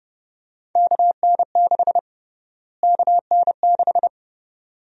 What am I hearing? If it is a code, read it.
Morse code, heard as a keyed tone, about 30 wpm, 700 Hz. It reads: KN6 KN6
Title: USA Call Sign Prefixes - Repeated 2x - 30wpm